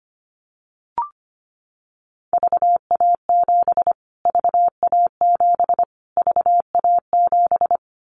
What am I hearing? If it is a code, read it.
Morse code, heard as a keyed tone, about 25 wpm, 700 Hz. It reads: VA7 VA7 VA7
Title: Callsign Prefixes for Top 25 DXCC - Repeated 3x - 25wpm